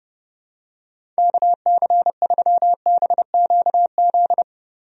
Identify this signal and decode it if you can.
Morse code, heard as a keyed tone, about 30 wpm, 700 Hz. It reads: KC3BQZ
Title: USA Call Signs - 6 Characters - Repeated 2x - 30wpm